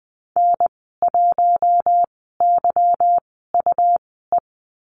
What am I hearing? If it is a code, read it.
Morse code, heard as a keyed tone, about 20 wpm, spaced as for 14 wpm, 700 Hz. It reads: N1YUE
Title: 161 Lesson 26g - Letter g - Call Signs 20wpm